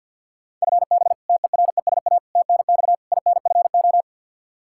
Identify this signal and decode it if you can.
Morse code, heard as a keyed tone, about 52 wpm, 700 Hz. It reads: FB TERESA TNX INFO
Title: Practice QSO - Exchange 3 - More Info - 50wpm